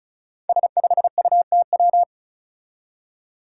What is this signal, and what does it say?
Morse code, heard as a keyed tone, about 34 wpm, 700 Hz. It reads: S5UTW
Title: Five Challenging Letter-Number Combinations 35wpm